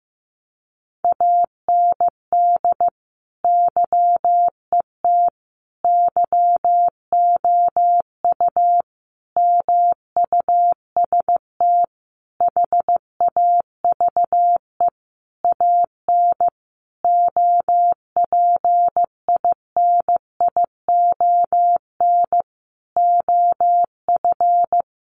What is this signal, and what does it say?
Morse code, heard as a keyed tone, about 15 wpm, 700 Hz. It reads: AND YET YOU MUST HAVE AN OPINION OF
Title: Sentences from Top 1300 Words 15wpm